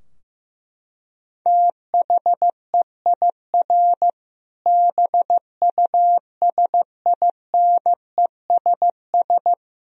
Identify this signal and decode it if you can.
Morse code, heard as a keyed tone, about 15 wpm, 700 Hz. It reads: THEIR BUSINESS
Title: Sentences from Top 1000 Words - Review Part 3 - 15wpm